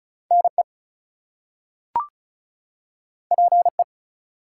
Morse code, heard as a keyed tone, about 36 wpm, 700 Hz. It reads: NE PE
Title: US States and Canadian Provinces 35wpm